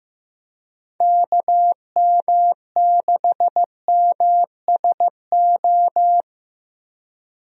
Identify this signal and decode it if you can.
Morse code, heard as a keyed tone, about 15 wpm, 700 Hz. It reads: KM6MSO
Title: USA Call Signs - 6 Characters - 15wpm